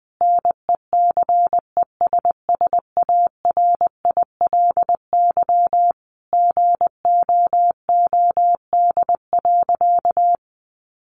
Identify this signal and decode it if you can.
Morse code, heard as a keyed tone, about 20 wpm, 700 Hz. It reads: NECESSARILY GOOD.
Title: Sentences from Top 1400 Words 20wpm